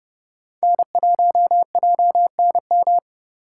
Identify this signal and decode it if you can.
Morse code, heard as a keyed tone, about 38 wpm, 700 Hz. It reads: N1JNM
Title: Speed Racing - USA Call Signs 30wpm